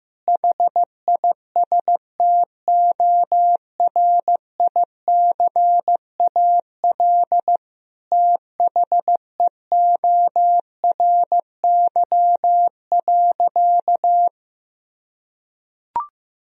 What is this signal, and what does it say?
Morse code, heard as a keyed tone, about 15 wpm, 700 Hz. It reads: HISTORICAL THEORY.
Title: Sentences from Top 1000 Words - Review Part 4 - 15wpm